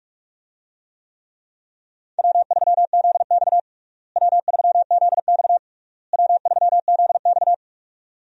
Morse code, heard as a keyed tone, about 46 wpm, 700 Hz. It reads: W3ZX W3ZX W3ZX
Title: USA Call Signs - 4 Characters - Repeated 3x - 45wpm